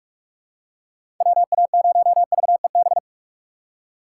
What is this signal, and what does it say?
Morse code, heard as a keyed tone, about 46 wpm, 700 Hz. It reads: WA0VEB
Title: USA Call Signs - 6 Characters - 8x Character Spacing - 45wpm